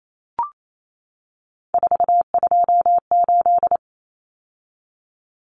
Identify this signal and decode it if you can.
Morse code, heard as a keyed tone, about 28 wpm, 700 Hz. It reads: 428
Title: Common Cut Numbers - 3-Digit 28wpm